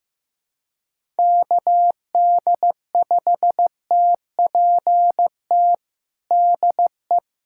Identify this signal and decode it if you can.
Morse code, heard as a keyed tone, about 15 wpm, 700 Hz. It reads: KD5TPT DE
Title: USA Callsign DE Callsign 15wpm